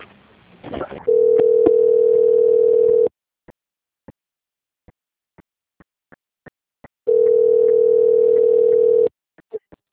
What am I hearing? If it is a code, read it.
Ringback tone (North America), 440 + 480 Hz.